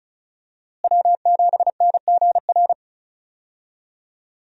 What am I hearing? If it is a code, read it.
Morse code, heard as a keyed tone, about 34 wpm, 700 Hz. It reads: W7NGR
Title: USA Call Signs - 5 Characters - Repeated 2x - 35wpm